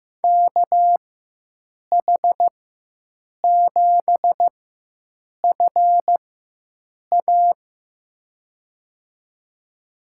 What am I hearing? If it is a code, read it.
Morse code, heard as a keyed tone, about 15 wpm, spaced as for 7 wpm, 700 Hz. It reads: KH7FA
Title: USA Call Signs - 5 Characters - 4x Character Spacing - 15wpm